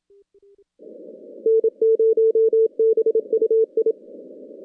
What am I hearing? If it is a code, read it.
Morse code, heard as a keyed tone, about 28 wpm, 440 Hz. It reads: N0BUI
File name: contests / CW-SS-k5tr- -2100